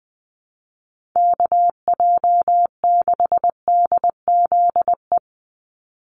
Morse code, heard as a keyed tone, about 20 wpm, 700 Hz. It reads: KJ6DZE
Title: USA Call Signs 20wpm